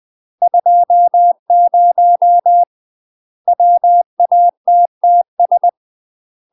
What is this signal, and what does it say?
Morse code, heard as a keyed tone, about 20 wpm, 700 Hz. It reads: 20 WATTS
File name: UK MASTER QSOs TRAINING / 20WPM / UK MASTER QSOs 0017